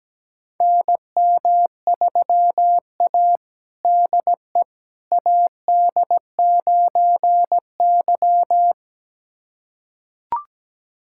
Morse code, heard as a keyed tone, about 17 wpm, 700 Hz. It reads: NM3A DE AD9Y
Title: USA Callsign DE Callsign - 4 Characters - 17wpm